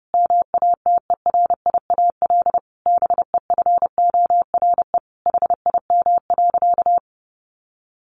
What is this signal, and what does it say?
Morse code, heard as a keyed tone, about 30 wpm, 700 Hz. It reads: MATERIAL BEFORE HIM.
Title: Sentences from Top 1400 Words 30wpm